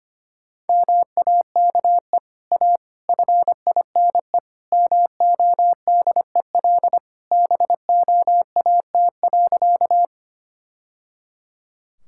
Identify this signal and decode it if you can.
Morse code, heard as a keyed tone, about 25 wpm, 700 Hz. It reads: MAKE A FINE MODEL BOAT.
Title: Sentences from Top 1000 Words - Review Part 5 - 25wpm